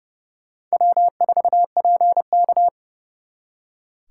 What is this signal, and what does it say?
Morse code, heard as a keyed tone, about 37 wpm, 700 Hz. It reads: W4PK
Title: Speed Racing - USA 1x2 Call Signs 30wpm